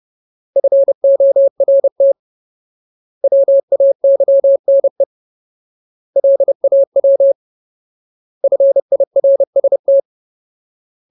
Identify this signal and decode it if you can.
Morse code, heard as a keyed tone, about 30 wpm, 550 Hz. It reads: FORT WAYNE LAW FIRST
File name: Ultimate Word List / 30WPM Ultimate